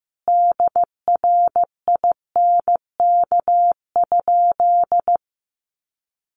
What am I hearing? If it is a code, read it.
Morse code, heard as a keyed tone, about 15 wpm, 700 Hz. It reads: DRINK?
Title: Sentences from Top 1500 Words 15wpm